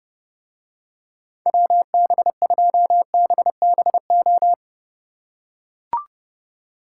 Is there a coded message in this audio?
Yes: Morse code from 1.5 to 4.5 s, heard as a keyed tone, about 30 wpm, 700 Hz. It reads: WB2BBO